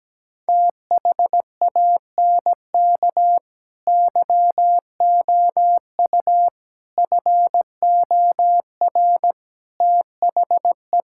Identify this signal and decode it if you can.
Morse code, heard as a keyed tone, about 17 wpm, 700 Hz. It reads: THANK YOU FOR THE
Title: Sentences from Top 1600 Words 17wpm